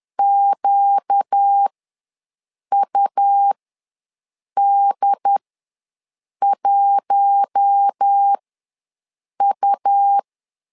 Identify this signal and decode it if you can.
Morse code, heard as a keyed tone, about 10 wpm, spaced as for 6 wpm, 800 Hz. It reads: QUD1U
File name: LEZIONI / 4 Q1 ws 10wpm 2538